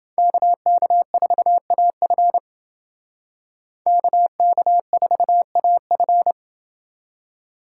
Morse code, heard as a keyed tone, about 29 wpm, 700 Hz. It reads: KK4AF KK4AF